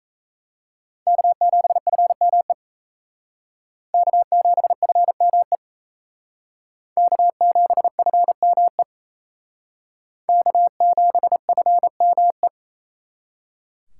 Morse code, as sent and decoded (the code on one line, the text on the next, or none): -.- --... ..-. -- . / -.- --... ..-. -- . / -.- --... ..-. -- . / -.- --... ..-. -- .
K7FME K7FME K7FME K7FME